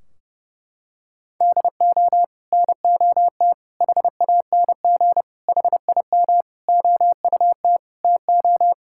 Morse code, heard as a keyed tone, about 32 wpm, 700 Hz. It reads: DO NOT HANG HIM OUT TO
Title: Sentences from Top 1600 Words 30wpm